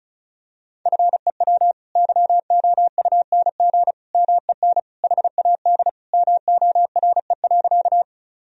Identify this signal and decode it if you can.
Morse code, heard as a keyed tone, about 36 wpm, 700 Hz. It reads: FEW YOUNG MEN HAD MORE.
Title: Sentences from Top 1000 Words - Review Part 1 - 35wpm